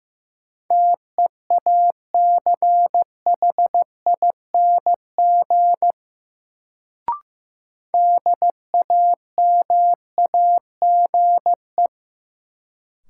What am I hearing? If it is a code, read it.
Morse code, heard as a keyed tone, about 15 wpm, 700 Hz. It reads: TEACHING DAMAGE
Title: Sentences from Top 1100 Words - New Word Drill - 15wpm